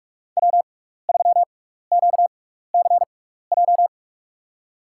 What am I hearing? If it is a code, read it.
Morse code, heard as a keyed tone, about 48 wpm, spaced as for 16 wpm, 700 Hz. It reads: W3QCJ
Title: USA Call Signs - 5 Characters - 6x Character Spacing - 45wpm